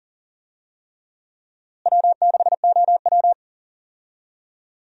Morse code, heard as a keyed tone, about 39 wpm, 700 Hz. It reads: W6OW
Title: USA Call Signs - 4 Characters - Repeated 3x - 40wpm